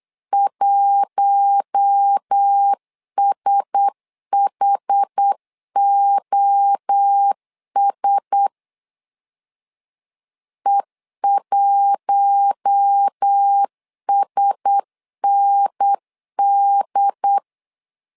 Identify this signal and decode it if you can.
Morse code, heard as a keyed tone, about 8 wpm, 800 Hz. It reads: 1SHOS E1SND
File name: LEZIONI / 4 Q1S 8wpm 2710